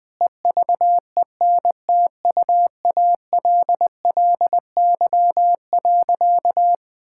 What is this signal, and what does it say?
Morse code, heard as a keyed tone, about 20 wpm, 700 Hz. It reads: EVENTUALLY.
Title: Sentences with 8-Word Phrases - 20wpm